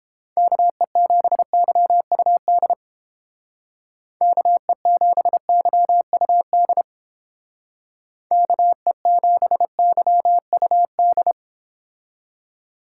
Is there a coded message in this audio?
Yes: Morse code from 0.4 to 11.3 s, heard as a keyed tone, about 30 wpm, 700 Hz. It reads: KE7YUD KE7YUD KE7YUD